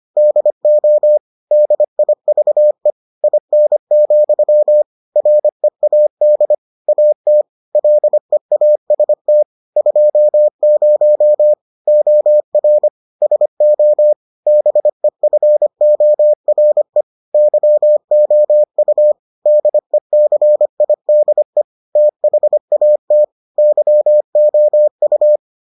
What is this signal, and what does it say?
Morse code, heard as a keyed tone, about 25 wpm, 600 Hz. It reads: DO DIVE IN, READ AT LEAST 20 OR SO BEFORE YOU DECIDE THAT YOU